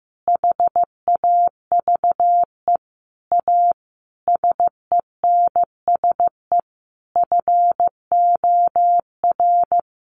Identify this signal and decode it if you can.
Morse code, heard as a keyed tone, about 15 wpm, 700 Hz. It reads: HAVE A SENSE FOR
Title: Sentences from Top 700 Words 15wpm